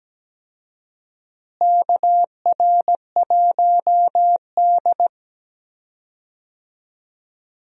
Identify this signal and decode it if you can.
Morse code, heard as a keyed tone, about 17 wpm, 700 Hz. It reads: KR1D
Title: USA Call Signs - 4 Characters - Repeated 3x - 17wpm